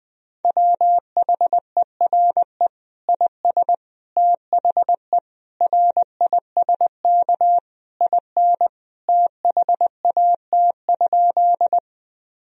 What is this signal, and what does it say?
Morse code, heard as a keyed tone, about 20 wpm, 700 Hz. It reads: WHERE IS THE RISK IN THAT?
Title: Sentences from Top 1000 Words - Review Part 4 - 20wpm